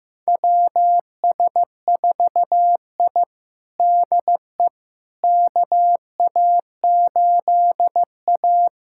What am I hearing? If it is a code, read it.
Morse code, heard as a keyed tone, about 16 wpm, 700 Hz. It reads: WS4I DE KA8A
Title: USA Callsign DE Callsign - 4 Characters - 15wpm